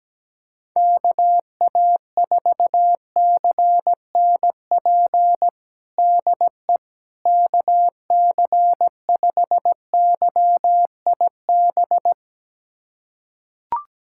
Morse code, heard as a keyed tone, about 17 wpm, 700 Hz. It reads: KA4CNP DE KC5YIB
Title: USA Callsign DE Callsign - 6 Characters - 17wpm